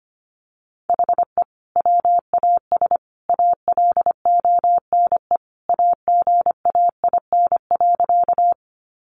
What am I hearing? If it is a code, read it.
Morse code, heard as a keyed tone, about 25 wpm, 700 Hz. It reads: HE WAS ALONE AGAIN.
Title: Sentences from Top 1000 Words 25wpm